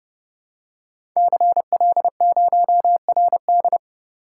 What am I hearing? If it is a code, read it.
Morse code, heard as a keyed tone, about 30 wpm, 700 Hz. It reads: CL0RD
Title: Code Groups - 5 Letter:Numbers - 30wpm